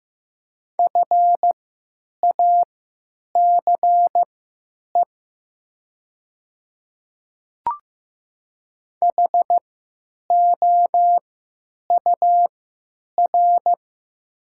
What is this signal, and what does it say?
Morse code, heard as a keyed tone, about 15 wpm, spaced as for 9 wpm, 700 Hz. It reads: FACE HOUR
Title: 3x Character Spacing - Top 500 Words - 15wpm